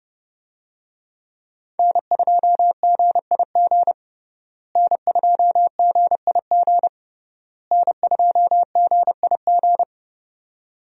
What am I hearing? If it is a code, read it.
Morse code, heard as a keyed tone, about 30 wpm, 700 Hz. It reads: N2GIG N2GIG N2GIG
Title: USA Call Signs - 5 Characters - Repeated 3x - 30wpm